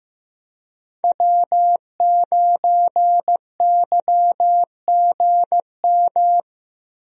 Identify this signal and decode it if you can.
Morse code, heard as a keyed tone, about 16 wpm, 700 Hz. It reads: W9YGM
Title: USA Call Signs - 5 Characters - Repeated 3x - 15wpm